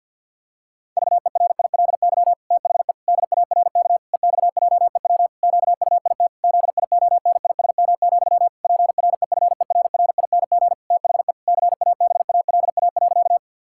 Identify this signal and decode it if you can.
Morse code, heard as a keyed tone, about 52 wpm, 700 Hz. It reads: VERILY THE DARK EXJEW QUIT ZIONISM, PREFERRING THE CABALA.
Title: Pangram Challenge v2 - 50wpm